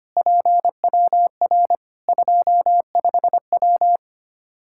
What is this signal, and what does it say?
Morse code, heard as a keyed tone, about 25 wpm, 700 Hz. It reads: PWR 25W
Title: Practice QSO - Exchange 2 - Info - 25wpm